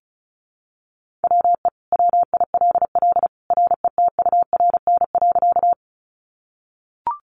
Morse code, heard as a keyed tone, about 34 wpm, 700 Hz. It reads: WE WILL RETURN.
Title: Sentences from Top 800 Words 35wpm